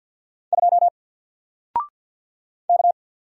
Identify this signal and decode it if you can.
Morse code, heard as a keyed tone, about 50 wpm, 700 Hz. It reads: 2 K
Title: Single Letter-Number 50wpm